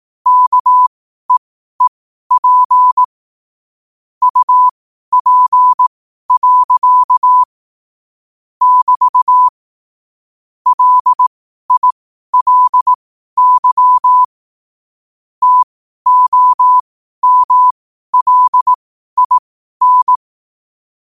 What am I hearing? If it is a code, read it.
Morse code, heard as a keyed tone, about 18 wpm, spaced as for 12 wpm, 1000 Hz. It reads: KEEP UP. <BT> LILY TOMLIN
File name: qotdfiles / 12wpm